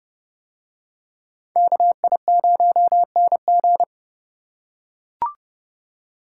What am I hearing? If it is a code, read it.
Morse code, heard as a keyed tone, about 28 wpm, 700 Hz. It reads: KI0NG